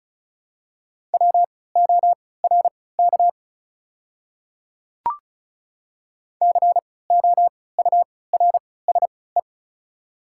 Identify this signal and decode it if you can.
Morse code, heard as a keyed tone, about 36 wpm, spaced as for 20 wpm, 700 Hz. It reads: WORK COURSE
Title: 3x Character Spacing - Top 500 Words - 35wpm